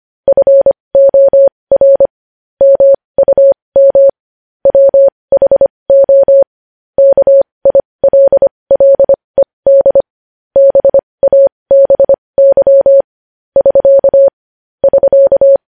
Morse code, heard as a keyed tone, about 25 wpm, spaced as for 20 wpm, 550 Hz. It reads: FOR MUM WHO KILLED BABY <SK> <SK>